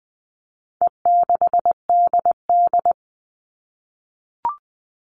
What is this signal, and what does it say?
Morse code, heard as a keyed tone, about 20 wpm, 700 Hz. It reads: E6DD
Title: All Cut Numbers - 4-Digit 20wpm